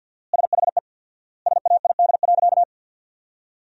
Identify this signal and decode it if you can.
Morse code, heard as a keyed tone, about 53 wpm, 700 Hz. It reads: SHE SAID.
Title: 4x Word-Spacing - Sentences from Top 100 Words - 50wpm